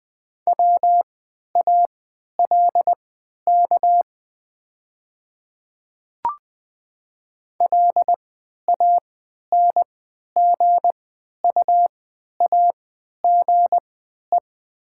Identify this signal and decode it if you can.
Morse code, heard as a keyed tone, about 20 wpm, spaced as for 11 wpm, 700 Hz. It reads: WALK LANGUAGE
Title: 3x Character Spacing - Top 500 Words - 20wpm